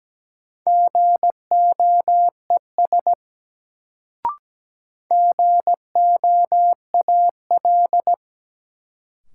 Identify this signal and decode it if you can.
Morse code, heard as a keyed tone, about 17 wpm, 700 Hz. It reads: GOES GOAL
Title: Sentences from Top 1000 Words - Review Part 5 - New Word Drill - 17wpm